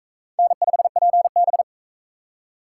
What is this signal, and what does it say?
Morse code, heard as a keyed tone, about 37 wpm, 700 Hz. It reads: N5PB